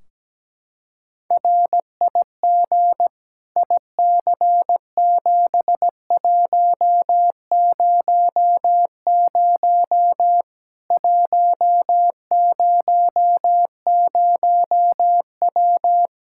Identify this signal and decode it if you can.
Morse code, heard as a keyed tone, about 17 wpm, 700 Hz. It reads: RIG IC7100 100W
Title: Practice QSO - Exchange 2 - Info - 17wpm